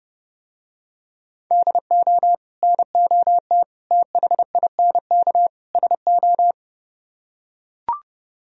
Morse code, heard as a keyed tone, about 30 wpm, 700 Hz. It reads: DO NOT THINK SO